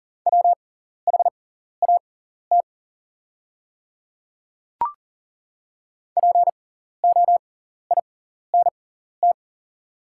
Morse code, heard as a keyed tone, about 40 wpm, spaced as for 14 wpm, 700 Hz. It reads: WHAT POINT
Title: 6x Character Spacing - Top 300 Words - 40wpm